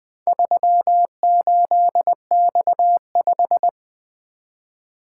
Morse code, heard as a keyed tone, about 20 wpm, 700 Hz. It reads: 38X5
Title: Four Letter-Number Combinations - 20wpm